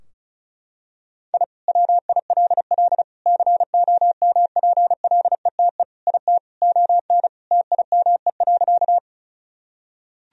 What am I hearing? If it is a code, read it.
Morse code, heard as a keyed tone, about 34 wpm, 700 Hz. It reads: I WILL COMPLETE IT ON TIME.
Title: Sentences from Top 1000 Words - Review Part 2 - 35wpm